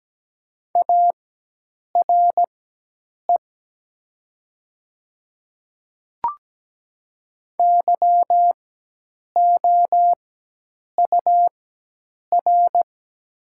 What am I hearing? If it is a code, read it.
Morse code, heard as a keyed tone, about 17 wpm, spaced as for 8 wpm, 700 Hz. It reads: ARE YOUR
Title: 4x Character Spacing - Top 100 Words - 17wpm